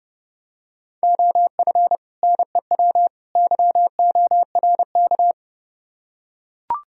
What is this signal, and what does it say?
Morse code, heard as a keyed tone, about 28 wpm, 700 Hz. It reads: OF NEW YORK